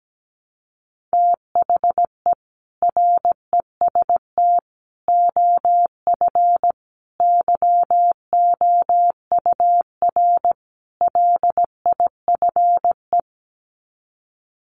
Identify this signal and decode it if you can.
Morse code, heard as a keyed tone, about 17 wpm, 700 Hz. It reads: THE REST OF YOUR LIFE
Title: Sets of 5 Words 17wpm